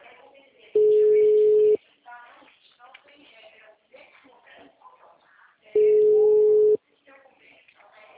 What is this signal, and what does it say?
Ringback tone (Europe, 425 Hz), with speech.